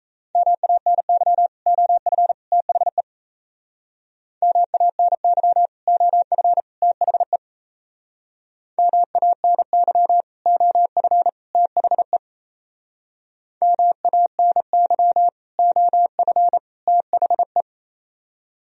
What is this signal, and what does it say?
Morse code, heard as a keyed tone, about 35 wpm, 700 Hz. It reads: MANY OF THE MANY OF THE MANY OF THE MANY OF THE
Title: Speed-Racing - Sets of 3 Words 28wpm